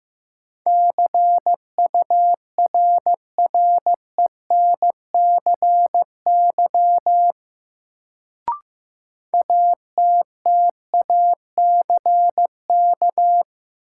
Morse code, heard as a keyed tone, about 15 wpm, 700 Hz. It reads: CURRENCY ATTACK
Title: Sentences from Top 1500 Words - New Word Drill - 15wpm